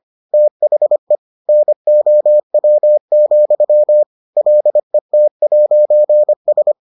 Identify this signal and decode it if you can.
Morse code, heard as a keyed tone, about 25 wpm, 600 Hz. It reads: THE NOW, LET'S